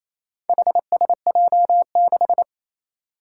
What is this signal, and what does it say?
Morse code, heard as a keyed tone, about 29 wpm, 700 Hz. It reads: HSJ6
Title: Four Challenging Letter-Number Combinations 28wpm